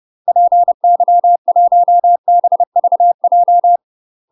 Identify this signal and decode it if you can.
Morse code, heard as a keyed tone, about 30 wpm, 700 Hz. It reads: PY1BVJ
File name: UK MASTER QSOs TRAINING/30WPM